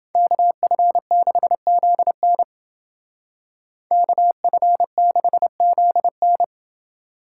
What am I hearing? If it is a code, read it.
Morse code, heard as a keyed tone, about 31 wpm, 700 Hz. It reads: KF6ZN KF6ZN